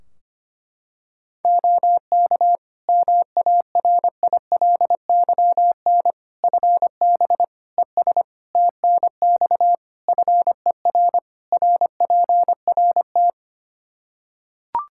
Morse code, heard as a keyed tone, about 25 wpm, 700 Hz. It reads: OK MARILYN FB ES TNX FER RPRT
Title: Practice QSO - Exchange 2 - Info - 25wpm